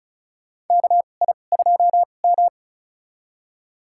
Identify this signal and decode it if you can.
Morse code, heard as a keyed tone, about 36 wpm, spaced as for 25 wpm, 690 Hz. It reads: KI2M
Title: USA Call Signs - 4 Characters - 2x Character Spacing - 35wpm